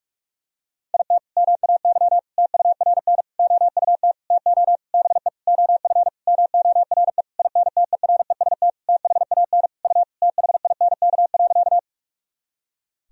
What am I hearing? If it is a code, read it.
Morse code, heard as a keyed tone, about 46 wpm, 700 Hz. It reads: IT MAY TURN OUT TO BE OF MORE INTEREST THAN U THINK.
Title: Sentences from Top 400 Words and Abbreviations 45wpm